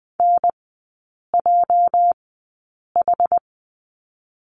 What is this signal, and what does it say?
Morse code, heard as a keyed tone, about 20 wpm, spaced as for 8 wpm, 690 Hz. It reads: NJH
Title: 205 Lesson 33e - Letter j - Three Characters 20wpm